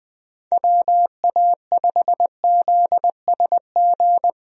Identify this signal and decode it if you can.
Morse code, heard as a keyed tone, about 20 wpm, 700 Hz. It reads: WA5ZSG
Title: USA Call Signs - 6 Characters - 20wpm